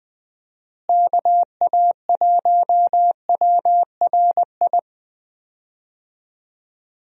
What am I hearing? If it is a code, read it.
Morse code, heard as a keyed tone, about 21 wpm, 700 Hz. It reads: KA1WRI
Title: USA Call Signs - 6 Characters - Repeated 2x - 20wpm